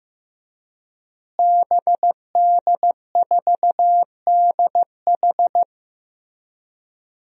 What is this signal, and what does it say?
Morse code, heard as a keyed tone, about 15 wpm, 700 Hz. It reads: BD4DH
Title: DXCC-100 Call Signs - 5 Characters - Repeated 2x - 15wpm